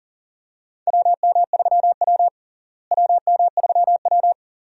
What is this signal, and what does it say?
Morse code, heard as a keyed tone, about 40 wpm, 700 Hz. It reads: WM3W WM3W
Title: USA Call Signs - 4 Characters - Repeated 2x - 40wpm